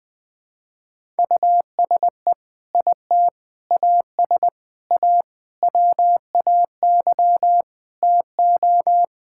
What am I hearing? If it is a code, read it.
Morse code, heard as a keyed tone, about 20 wpm, 700 Hz. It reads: USE IT AS A WAY TO
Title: Sentences with 4-Word Phrases - Spoken Beginning - 20wpm